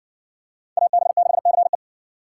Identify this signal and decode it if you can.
Morse code, heard as a keyed tone, about 56 wpm, 700 Hz. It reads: AB6CE